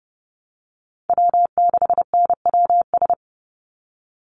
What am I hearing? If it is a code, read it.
Morse code, heard as a keyed tone, about 30 wpm, 700 Hz. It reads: W6NWS
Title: Call Signs Repeated 2x - 30wpm